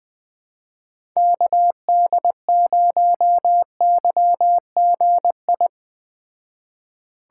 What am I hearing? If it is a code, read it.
Morse code, heard as a keyed tone, about 19 wpm, 700 Hz. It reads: KD0YGI
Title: Speed Racing - USA 2x3 Call Signs 17wpm